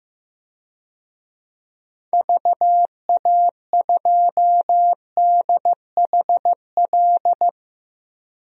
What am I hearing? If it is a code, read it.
Morse code, heard as a keyed tone, about 15 wpm, 700 Hz. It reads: VA2DHL
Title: DXCC-100 Call Signs - 6 Characters - 15wpm